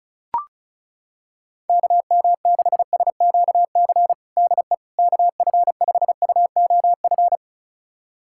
Morse code, heard as a keyed tone, about 36 wpm, 700 Hz. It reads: KM6SQC DE KF5UOF
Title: USA Callsign DE Callsign - 6 Characters - 35wpm